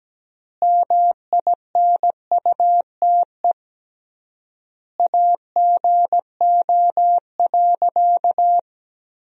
Morse code, heard as a keyed tone, about 17 wpm, 700 Hz. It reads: MINUTE AGO.
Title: 3x Word-Spacing - Sentences from Top 500 Words - 17wpm